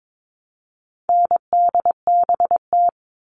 Morse code, heard as a keyed tone, about 22 wpm, 700 Hz. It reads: NDBT
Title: All Cut Numbers - 4-Digit 22wpm